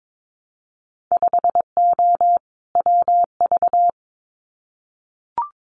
Morse code, heard as a keyed tone, about 22 wpm, 700 Hz. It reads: 5O WV
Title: Winter Field Day Exchanges Encore 22wpm